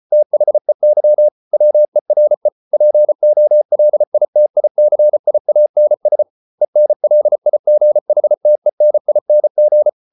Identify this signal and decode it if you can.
Morse code, heard as a keyed tone, about 34 wpm, 600 Hz. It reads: THEY WERE POLITICIANS ENLIGHTENING